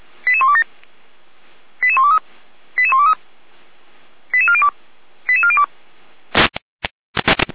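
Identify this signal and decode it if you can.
Five-tone selective call (ZVEI), heard as quick runs of five tones, tones of about 70 ms. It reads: 88127 88123 88123 88402 88402
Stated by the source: ZVEI five-tone selective call received by SDR for Feuerwehr Orlamünde fire brigade.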